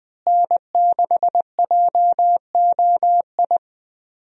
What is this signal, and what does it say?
Morse code, heard as a keyed tone, about 20 wpm, 700 Hz. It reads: N6JOI
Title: USA Call Signs - 5 Characters - Repeated 2x - 20wpm